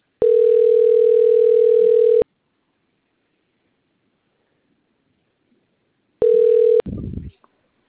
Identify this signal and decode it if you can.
Ringback tone (North America), 440 + 480 Hz.